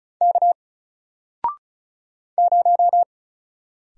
Morse code, heard as a keyed tone, about 36 wpm, 700 Hz. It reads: K 0
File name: Single-Letter-Number-35wpm